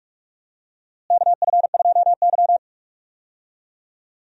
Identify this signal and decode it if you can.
Morse code, heard as a keyed tone, about 46 wpm, 700 Hz. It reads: KF2Y
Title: USA Call Signs - 4 Characters - Repeated 2x - 45wpm